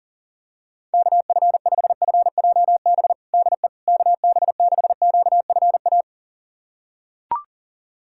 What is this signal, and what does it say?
Morse code, heard as a keyed tone, about 40 wpm, 700 Hz. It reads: KF5FJB DE KB6QFA
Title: USA Callsign DE Callsign - 6 Characters - 40wpm